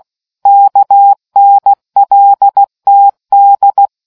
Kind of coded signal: Morse code